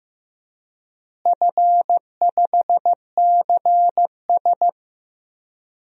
Morse code, heard as a keyed tone, about 15 wpm, 700 Hz. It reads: F5CS
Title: DXCC-100 Call Signs - 4 Characters - Repeated 3x - 15wpm